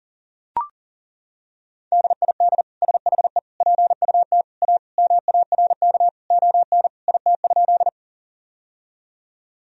Morse code, heard as a keyed tone, about 37 wpm, 700 Hz. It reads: DID SHE PUT A MARK ON IT?
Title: Sentences from Top 1000 Words - Review Part 1 - 40wpm